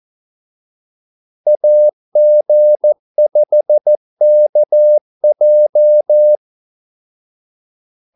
Morse code, heard as a keyed tone, about 14 wpm, 600 Hz. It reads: AG5KJ
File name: Callsigns-14wpm